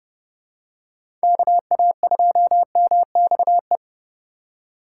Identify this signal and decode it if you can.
Morse code, heard as a keyed tone, about 30 wpm, 700 Hz. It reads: KA2MXE
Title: USA Call Signs - 6 Characters - 4x Character Spacing - 30wpm